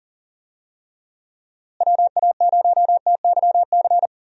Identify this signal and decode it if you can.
Morse code, heard as a keyed tone, about 38 wpm, 700 Hz. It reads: WA0TYC
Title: USA Call Signs - 6 Characters - Repeated 3x - 40wpm